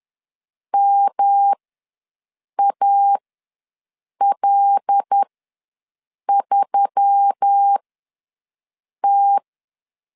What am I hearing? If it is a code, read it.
Morse code, heard as a keyed tone, about 10 wpm, spaced as for 6 wpm, 800 Hz. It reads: MAL3T
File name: LEZIONI / 5 LBVC / 5 L ws 10wpm 3636